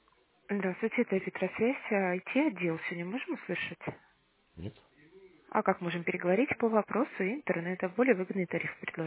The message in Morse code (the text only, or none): none